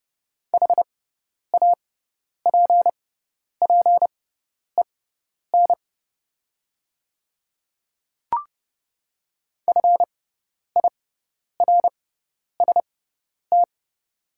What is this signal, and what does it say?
Morse code, heard as a keyed tone, about 30 wpm, spaced as for 10 wpm, 690 Hz. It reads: HAPPEN FIRST
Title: 6x Character Spacing - Top 500 Words - 30wpm